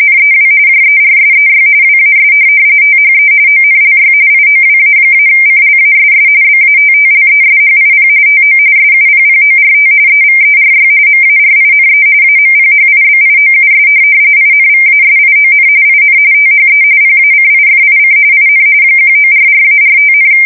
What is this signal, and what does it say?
RTTY, about 45.45 baud. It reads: NEW JERSEY WAS THE ONLY IOWA-CLASS BATTLESHIP TO SUPPORT EVERY MARINE AMPHIBIOUS INVASION IN THE PACIFIC IN 1944-45,
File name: fsk170 / USS-New-Jersey